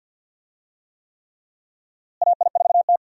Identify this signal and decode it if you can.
Morse code, heard as a keyed tone, about 52 wpm, 700 Hz. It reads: AI4T
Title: USA Call Signs - 4 Characters - Repeated 3x - 50wpm